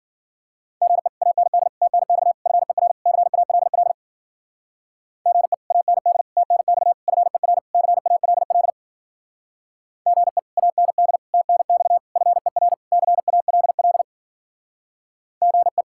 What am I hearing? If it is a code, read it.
Morse code, heard as a keyed tone, about 52 wpm, 700 Hz. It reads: GE AND TNX FER CALL GE AND TNX FER CALL GE AND TNX FER CALL GE